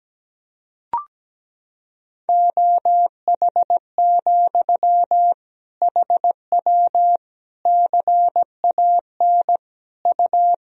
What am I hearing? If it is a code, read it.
Morse code, heard as a keyed tone, about 17 wpm, 700 Hz. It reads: OH, HW CAN U
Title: Sentences from Top 500 Words and Abbreviations 17wpm